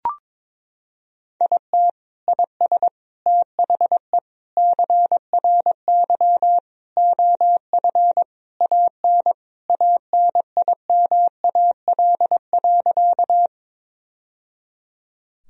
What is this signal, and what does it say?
Morse code, heard as a keyed tone, about 22 wpm, 700 Hz. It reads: IT IS THE CRY OF AN ANIMAL.
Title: Sentences from Top 1000 Words - Review Part 2 - 22wpm